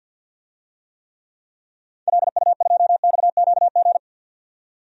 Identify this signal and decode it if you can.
Morse code, heard as a keyed tone, about 46 wpm, 700 Hz. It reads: LU2XQG
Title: DXCC-100 Call Signs - 6 Characters - 50wpm